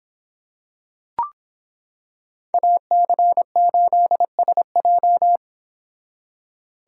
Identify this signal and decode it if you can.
Morse code, heard as a keyed tone, about 26 wpm, 700 Hz. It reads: AC8SJ